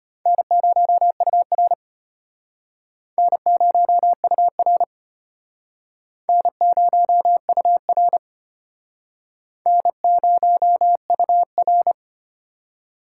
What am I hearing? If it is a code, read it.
Morse code, heard as a keyed tone, about 34 wpm, 700 Hz. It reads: N0UR N0UR N0UR N0UR